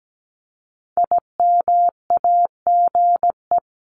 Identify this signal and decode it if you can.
Morse code, heard as a keyed tone, about 17 wpm, 700 Hz. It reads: IMAGE
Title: Top 501-750 Words 17wpm